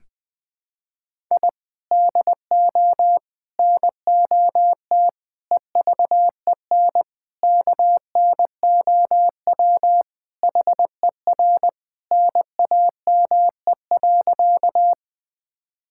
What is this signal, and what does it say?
Morse code, heard as a keyed tone, about 20 wpm, 700 Hz. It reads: I DO NOT EVEN KNOW HER NAME.
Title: Sentences with 5-Word Phrases - Spoken Beginning - 20wpm